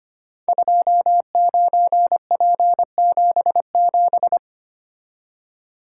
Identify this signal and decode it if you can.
Morse code, heard as a keyed tone, about 25 wpm, 700 Hz. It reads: 29P77
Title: Code Groups - 5 Letter:Numbers - 25wpm